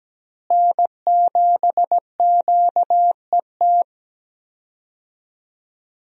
Morse code, heard as a keyed tone, about 17 wpm, 700 Hz. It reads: N7QET
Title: USA Call Signs - 5 Characters - Repeated 2x - 17wpm